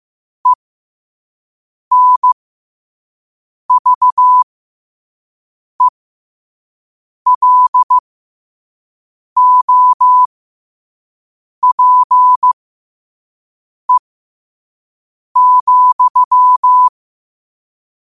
Morse code, heard as a keyed tone, about 15 wpm, spaced as for 5 wpm, 1000 Hz. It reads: ENVELOPE,
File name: qotdfiles / 5wpm